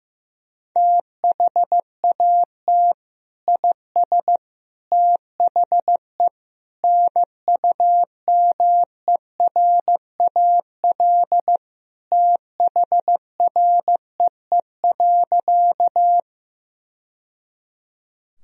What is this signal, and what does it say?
Morse code, heard as a keyed tone, about 15 wpm, 700 Hz. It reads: THAT IS THE NUMERAL THREE.
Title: Sentences from Top 1000 Words - Review Part 2 - 15wpm